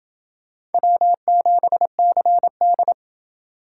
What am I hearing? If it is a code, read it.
Morse code, heard as a keyed tone, about 28 wpm, 700 Hz. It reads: W7CD